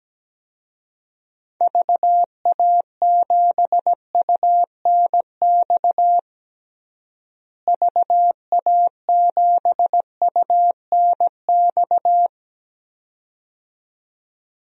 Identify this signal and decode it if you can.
Morse code, heard as a keyed tone, about 17 wpm, 700 Hz. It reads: VA7UNX VA7UNX
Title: DXCC-100 Call Signs - 6 Characters - Repeated 2x - 17wpm